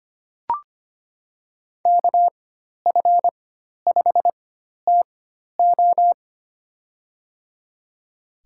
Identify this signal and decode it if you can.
Morse code, heard as a keyed tone, about 25 wpm, spaced as for 12 wpm, 700 Hz. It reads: KF5TO
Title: USA Call Signs - 5 Characters - 4x Character Spacing - 25wpm